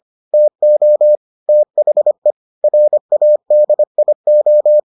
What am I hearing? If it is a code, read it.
Morse code, heard as a keyed tone, about 25 wpm, 600 Hz. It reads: TO THE RADIO